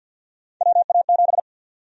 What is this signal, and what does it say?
Morse code, heard as a keyed tone, about 53 wpm, 700 Hz. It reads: WA7
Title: USA Call Sign Prefixes - Repeated 2x - 50wpm